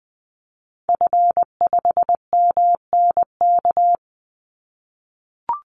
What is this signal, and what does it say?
Morse code, heard as a keyed tone, about 20 wpm, 700 Hz. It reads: F5MNK
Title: Call Signs - 20wpm